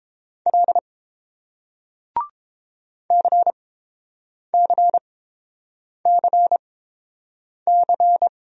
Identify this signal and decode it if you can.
Morse code, heard as a keyed tone, about 30 wpm, spaced as for 7 wpm, 700 Hz. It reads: L CCCC